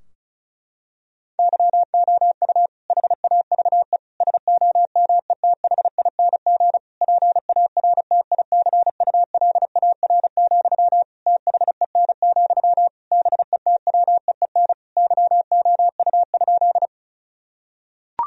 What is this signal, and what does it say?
Morse code, heard as a keyed tone, about 36 wpm, 700 Hz. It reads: YOU HAVE SOMETHING PARTICULAR, THEN, BETWEEN YOU?
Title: Sentences from Top 1000 Words - Review Part 4 - 35wpm